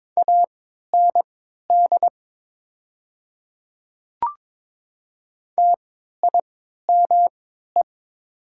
Morse code, heard as a keyed tone, about 22 wpm, spaced as for 12 wpm, 700 Hz. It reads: AND TIME
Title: 3x Character Spacing - Top 200 Words - 22wpm